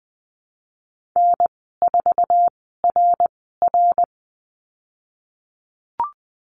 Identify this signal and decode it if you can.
Morse code, heard as a keyed tone, about 20 wpm, spaced as for 15 wpm, 700 Hz. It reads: N4RR
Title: 055 Lesson 10g - Letter r - Call Signs 20wpm